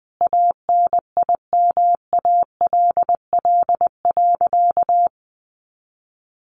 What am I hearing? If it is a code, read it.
Morse code, heard as a keyed tone, about 19 wpm, 700 Hz. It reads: ANIMALL.
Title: A Sound Of Thunder 20wpm